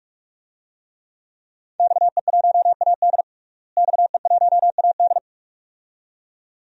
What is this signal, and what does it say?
Morse code, heard as a keyed tone, about 43 wpm, 700 Hz. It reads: XE1AD XE1AD
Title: DXCC-100 Call Signs - 5 Characters - Repeated 2x - 45wpm